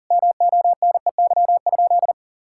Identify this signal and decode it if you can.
Morse code, heard as a keyed tone, about 37 wpm, 700 Hz. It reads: MONEY?
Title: Sentences from Top 1600 Words 40wpm